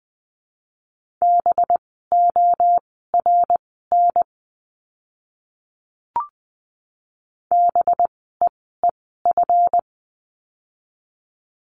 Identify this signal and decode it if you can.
Morse code, heard as a keyed tone, about 20 wpm, spaced as for 14 wpm, 700 Hz. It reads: BORN BEEF
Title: 185 Lesson 30f - Letter b - Words 20wpm